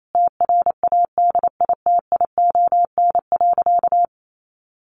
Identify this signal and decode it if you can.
Morse code, heard as a keyed tone, about 28 wpm, 700 Hz. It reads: TRADITION.
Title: Sentences from Top 1500 Words 28wpm